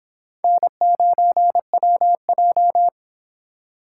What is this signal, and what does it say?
Morse code, heard as a keyed tone, about 25 wpm, 700 Hz. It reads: N9WJ